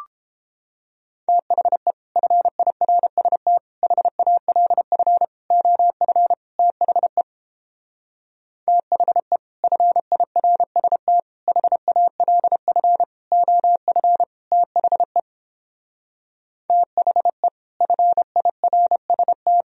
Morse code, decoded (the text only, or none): THE FIRST HALF OF THE THE FIRST HALF OF THE THE FIRST